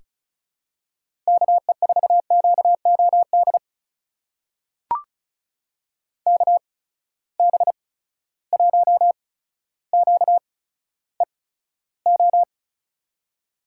Morse code, heard as a keyed tone, about 36 wpm, spaced as for 15 wpm, 700 Hz. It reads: KE4QOD KB1QEO
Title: USA Call Signs - 6 Characters - 8x Character Spacing - 35wpm